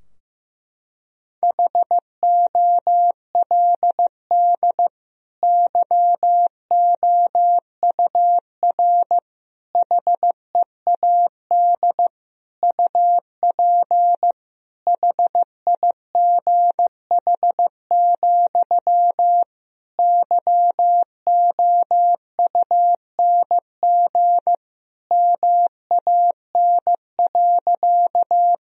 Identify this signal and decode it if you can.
Morse code, heard as a keyed tone, about 15 wpm, 700 Hz. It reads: HOLD YOUR HEAD UP HIGH, YOUNG MAN.
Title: Sentences from Top 1000 Words - Review Part 2 - 15wpm